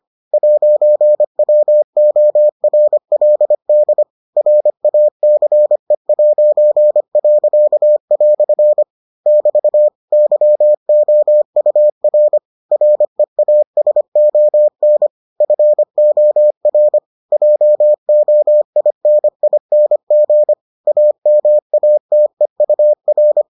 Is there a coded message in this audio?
Yes: Morse code from 0.3 to 23.4 s, heard as a keyed tone, about 25 wpm, 600 Hz. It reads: 'WORLD RACE'." <BT> YOUR REASON FOR JOINING AMATEUR